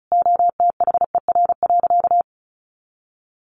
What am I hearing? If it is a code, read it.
Morse code, heard as a keyed tone, about 34 wpm, 700 Hz. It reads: OTHER.
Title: Sentences from Top 1300 Words 35wpm